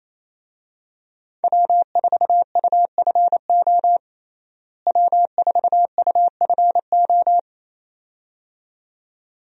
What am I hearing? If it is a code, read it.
Morse code, heard as a keyed tone, about 27 wpm, 700 Hz. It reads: W4UFO W4UFO
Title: USA Call Signs - 5 Characters - Repeated 2x - 28wpm